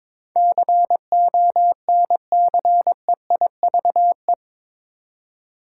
Morse code, heard as a keyed tone, about 22 wpm, 700 Hz. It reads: CONCEIVE
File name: practice / 22wpm / Eight Letter Words - 22wpm